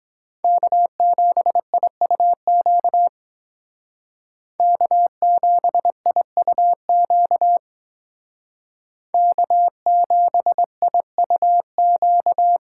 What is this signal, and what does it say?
Morse code, heard as a keyed tone, about 22 wpm, 700 Hz. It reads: K7IUQ K7IUQ K7IUQ